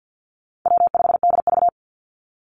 Morse code, heard as a keyed tone, about 50 wpm, 700 Hz. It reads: W5DV
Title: Call Signs - 50wpm